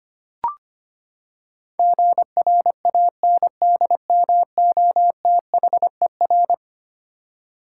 Morse code, heard as a keyed tone, about 25 wpm, 700 Hz. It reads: GRANDMOTHER
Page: morse-code-ninja-practice-25wpm